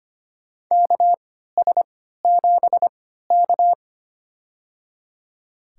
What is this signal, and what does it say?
Morse code, heard as a keyed tone, about 25 wpm, spaced as for 14 wpm, 700 Hz. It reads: KS7K
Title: USA Call Signs - 4 Characters - 3x Character Spacing - 25wpm